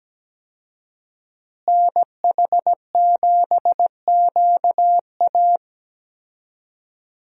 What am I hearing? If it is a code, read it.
Morse code, heard as a keyed tone, about 17 wpm, 700 Hz. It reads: NH7QA
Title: USA Call Signs - 5 Characters - Repeated 2x - 17wpm